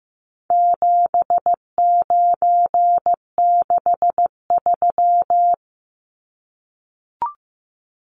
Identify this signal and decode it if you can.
Morse code, heard as a keyed tone, about 15 wpm, 700 Hz. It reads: 7963
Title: Numbers - 4-Digit 15wpm